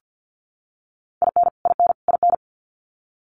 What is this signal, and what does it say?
Morse code, heard as a keyed tone, about 50 wpm, 700 Hz. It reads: IN IN IN
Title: US States Repeated 3x 50wpm